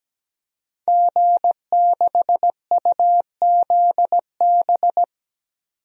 Morse code, heard as a keyed tone, about 17 wpm, 700 Hz. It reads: G6UZB
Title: DXCC-100 Call Signs - 5 Characters - 17wpm